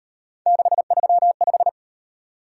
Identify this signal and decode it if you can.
Morse code, heard as a keyed tone, about 38 wpm, 700 Hz. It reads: 635